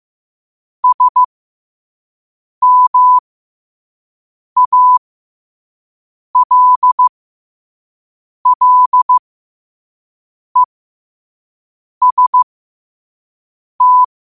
Morse code, heard as a keyed tone, about 15 wpm, spaced as for 5 wpm, 1000 Hz. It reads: SMALLEST
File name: 5wpm